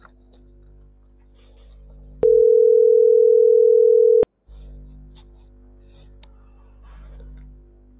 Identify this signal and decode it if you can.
Ringback tone (North America), 440 + 480 Hz.